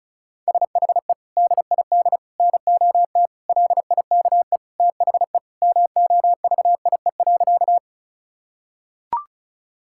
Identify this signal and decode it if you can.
Morse code, heard as a keyed tone, about 36 wpm, 700 Hz. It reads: SHE DID NOT LIKE THE MOVIE.
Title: Sentences from Top 1000 Words - Review Part 5 - 35wpm